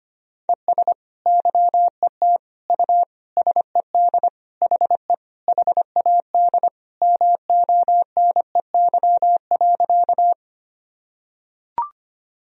Morse code, heard as a keyed tone, about 25 wpm, 700 Hz. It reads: ES YET U SED HE HAD MONEY.
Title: Sentences from Top 500 Words and Abbreviations 25wpm